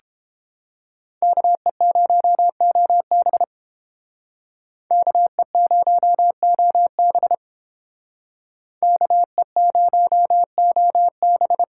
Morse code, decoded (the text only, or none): KE0OB KE0OB KE0OB